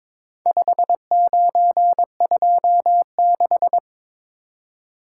Morse code, heard as a keyed tone, about 22 wpm, 700 Hz. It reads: 5926
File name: practice / 22wpm / Four Letter-Number Combinations - 22wpm